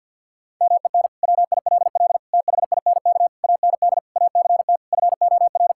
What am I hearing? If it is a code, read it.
Morse code, heard as a keyed tone, about 50 wpm, 700 Hz. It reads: MEN WILL THINK AND ACT FOR
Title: Sentences from Top 1000 Words - Review Part 5 - 50wpm